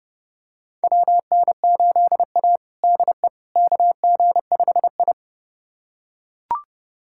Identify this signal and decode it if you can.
Morse code, heard as a keyed tone, about 30 wpm, 700 Hz. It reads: WN8A DE KG5I
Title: USA Callsign DE Callsign - 4 Characters - 30wpm